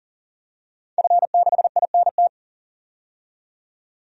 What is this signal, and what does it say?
Morse code, heard as a keyed tone, about 37 wpm, 700 Hz. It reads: F6INT
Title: DXCC-100 Call Signs - 5 Characters - 40wpm